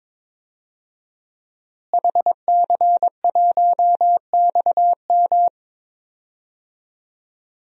Morse code, heard as a keyed tone, about 22 wpm, 700 Hz. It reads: HC1XM
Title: DXCC-100 Call Signs - 5 Characters - Repeated 3x - 22wpm